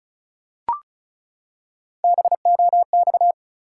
Morse code, heard as a keyed tone, about 36 wpm, 700 Hz. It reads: BOX
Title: Three Letter Words - 35wpm